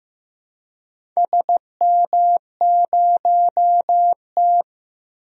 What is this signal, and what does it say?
Morse code, heard as a keyed tone, about 15 wpm, 700 Hz. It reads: SM0T
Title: DXCC-100 Call Signs - 4 Characters - 15wpm